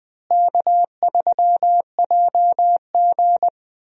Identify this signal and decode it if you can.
Morse code, heard as a keyed tone, about 20 wpm, 700 Hz. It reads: K3JG
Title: USA Call Signs - 4 Characters - Repeated 3x - 20wpm